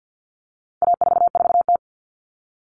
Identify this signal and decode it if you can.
Morse code, heard as a keyed tone, about 50 wpm, 700 Hz. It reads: A44T
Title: All Cut Numbers - 4-Digit 50wpm